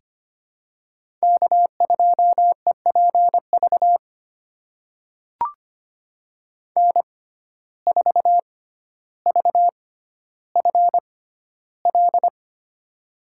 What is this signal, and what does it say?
Morse code, heard as a keyed tone, about 25 wpm, spaced as for 13 wpm, 700 Hz. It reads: K2EPV N4VFL
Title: USA Call Signs - 5 Characters - 6x Character Spacing - 25wpm